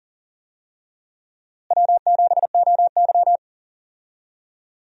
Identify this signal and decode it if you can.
Morse code, heard as a keyed tone, about 40 wpm, 700 Hz. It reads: W7OY
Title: USA Call Signs - 4 Characters - Repeated 3x - 40wpm